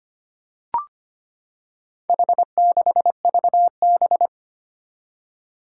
Morse code, heard as a keyed tone, about 25 wpm, 700 Hz. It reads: H6VB
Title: Four Challenging Letter-Number Combinations 25wpm